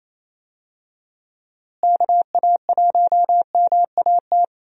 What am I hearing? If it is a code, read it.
Morse code, heard as a keyed tone, about 28 wpm, 700 Hz. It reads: KA1MAT
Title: USA Call Signs - 6 Characters - Repeated 3x - 28wpm